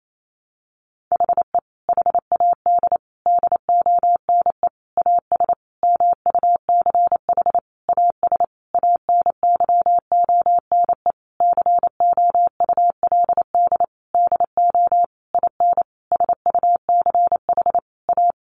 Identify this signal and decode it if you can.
Morse code, heard as a keyed tone, about 28 wpm, 700 Hz. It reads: HE HAD DONE AS MUCH AS ANYONE COULD DO IN SUCH A
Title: Sentences from Top 1200 Words 28wpm